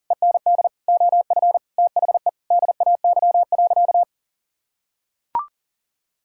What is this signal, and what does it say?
Morse code, heard as a keyed tone, about 37 wpm, 700 Hz. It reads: END OF THE DAY.
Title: Sentences from Top 1000 Words - Review Part 6 - 40wpm